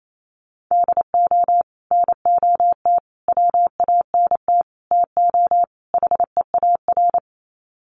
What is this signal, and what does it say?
Morse code, heard as a keyed tone, about 28 wpm, 700 Hz. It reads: DO NOT WANT TO HEAR
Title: Sets of 5 Words 28wpm